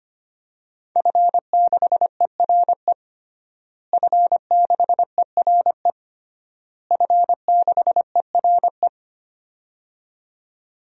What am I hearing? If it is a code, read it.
Morse code, heard as a keyed tone, about 25 wpm, 700 Hz. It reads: F6ERE F6ERE F6ERE
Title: DXCC-100 Call Signs - 5 Characters - Repeated 3x - 25wpm